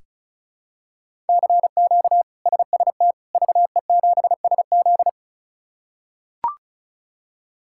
Morse code, heard as a keyed tone, about 34 wpm, 700 Hz. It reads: CQ SST VE7SZ
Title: K1USN SST QSO Practice - 35wpm